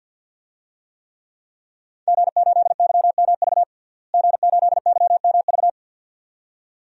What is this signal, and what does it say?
Morse code, heard as a keyed tone, about 49 wpm, 700 Hz. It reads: G8YMV G8YMV
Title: DXCC-100 Call Signs - 5 Characters - Repeated 2x - 50wpm